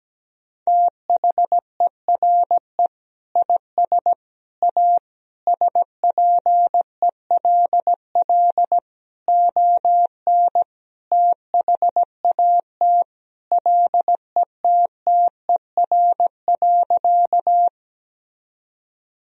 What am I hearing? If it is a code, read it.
Morse code, heard as a keyed tone, about 17 wpm, 700 Hz. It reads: THERE IS A SPELL ON THAT LETTER.
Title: Sentences from Top 1000 Words - Review Part 1 - 17wpm